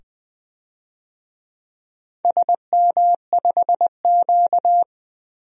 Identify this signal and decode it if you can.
Morse code, heard as a keyed tone, about 20 wpm, 700 Hz. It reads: SM5Q
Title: DXCC-100 Call Signs - 4 Characters - 20wpm